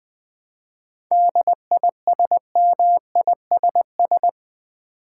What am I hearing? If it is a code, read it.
Morse code, heard as a keyed tone, about 20 wpm, 700 Hz. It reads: DISMISS
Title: 7 Letter Words - First 3 Letters Given - 20wpm